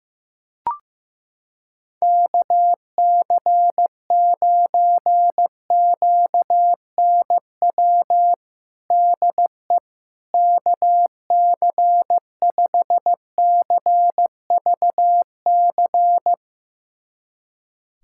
Morse code, heard as a keyed tone, about 15 wpm, 700 Hz. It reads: KC9QNW DE KC5CVC
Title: USA Callsign DE Callsign 15wpm